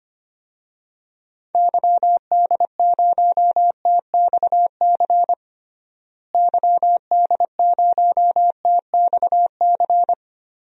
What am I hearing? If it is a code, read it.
Morse code, heard as a keyed tone, about 25 wpm, 700 Hz. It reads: YD0TXC YD0TXC
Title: DXCC-100 Call Signs - 6 Characters - Repeated 3x - 25wpm